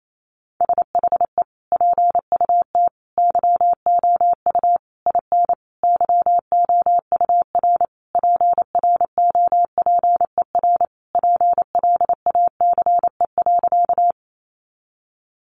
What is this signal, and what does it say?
Morse code, heard as a keyed tone, about 28 wpm, 700 Hz. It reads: SHE PUT YOU IN YOUR PROPER PLACE.
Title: Sentences from Top 800 Words 28wpm